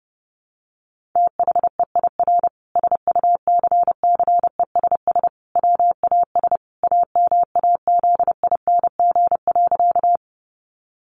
Morse code, heard as a keyed tone, about 28 wpm, 700 Hz. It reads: THEIR SUCCESS WAS AMAZING.
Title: Sentences from Top 1500 Words 30wpm